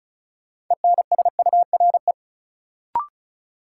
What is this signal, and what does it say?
Morse code, heard as a keyed tone, about 34 wpm, 700 Hz. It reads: ENSURE
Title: Sentences from Top 1100 Words - New Word Drill Alt - 35wpm